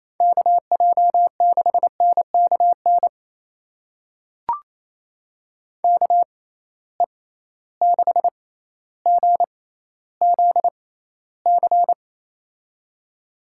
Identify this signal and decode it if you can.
Morse code, heard as a keyed tone, about 28 wpm, spaced as for 14 wpm, 700 Hz. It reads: KJ6NKN KE6GZC
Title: USA Call Signs - 6 Characters - 6x Character Spacing - 28wpm